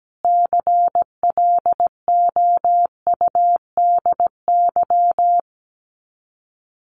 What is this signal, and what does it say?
Morse code, heard as a keyed tone, about 17 wpm, 700 Hz. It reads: CLOUDY
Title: Common QSO Phrases 17wpm